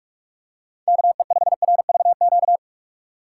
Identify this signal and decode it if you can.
Morse code, heard as a keyed tone, about 44 wpm, 700 Hz. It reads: KE5RVQ
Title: USA Call Signs - 6 Characters - 6x Character Spacing - 45wpm